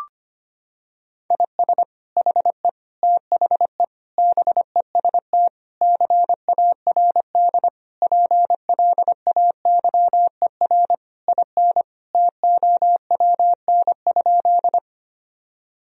Morse code, heard as a keyed tone, about 25 wpm, 690 Hz. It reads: IS HE THE BEST CARD PLAYER IN TOWN?
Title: Sentences from Top 1000 Words - Review Part 4 - 25wpm